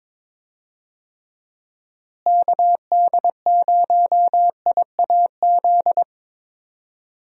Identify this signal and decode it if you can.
Morse code, heard as a keyed tone, about 21 wpm, 700 Hz. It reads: KD0IAZ
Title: USA Call Signs - 6 Characters - Repeated 2x - 22wpm